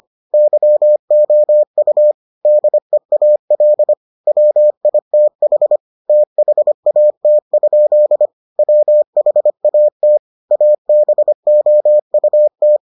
Morse code, heard as a keyed tone, about 25 wpm, 600 Hz. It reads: YOU DEAL WITH THAT? WHAT ABOUT